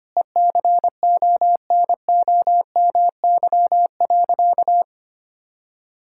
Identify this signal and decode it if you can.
Morse code, heard as a keyed tone, about 25 wpm, 700 Hz. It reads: ECONOMY.
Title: Sentences from Top 1000 Words - Review Part 5 - 25wpm